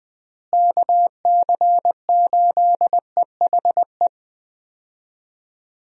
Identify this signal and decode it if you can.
Morse code, heard as a keyed tone, about 20 wpm, 700 Hz. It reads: KC8EHE
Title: USA Call Signs - 6 Characters - Repeated 3x - 20wpm